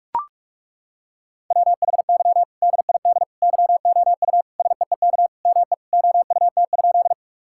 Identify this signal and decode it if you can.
Morse code, heard as a keyed tone, about 42 wpm, 700 Hz. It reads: WHY DID YOU SEEK ME OUT?
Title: Sentences from Top 1600 Words 45wpm